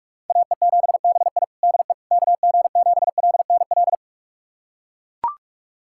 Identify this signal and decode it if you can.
Morse code, heard as a keyed tone, about 41 wpm, 700 Hz. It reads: AE7BI DE KG7LNL
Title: USA Callsign DE Callsign 45wpm